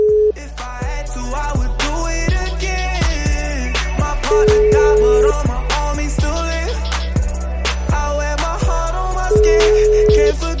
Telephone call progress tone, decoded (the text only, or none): ringback tone